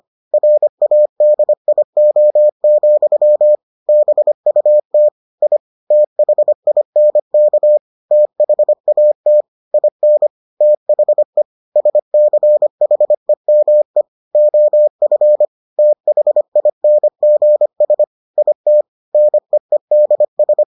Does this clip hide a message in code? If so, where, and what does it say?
Yes: Morse code from 0.3 to 20.6 s, heard as a keyed tone, about 25 wpm, 600 Hz. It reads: RADIO, BUT I THINK THAT IN THE SCHEME OF THINGS IT NEEDS